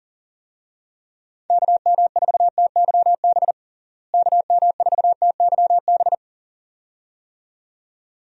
Morse code, heard as a keyed tone, about 40 wpm, 700 Hz. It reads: KM4TYB KM4TYB
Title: USA Call Signs - 6 Characters - Repeated 2x - 40wpm